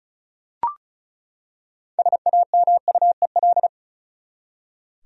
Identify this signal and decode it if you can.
Morse code, heard as a keyed tone, about 35 wpm, 700 Hz. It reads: SAMUEL
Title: USA Common Names - 35wpm